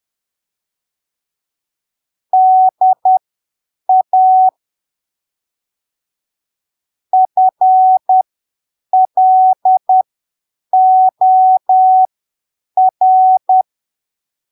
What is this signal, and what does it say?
Morse code, heard as a keyed tone, about 10 wpm, spaced as for 7 wpm, 750 Hz. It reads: DA FLOR
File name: arquivos / texto / texto 20 10ppm